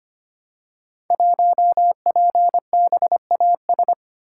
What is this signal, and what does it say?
Morse code, heard as a keyed tone, about 25 wpm, 700 Hz. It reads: 1PBAS
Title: Code Groups - 5 Letter:Numbers - 25wpm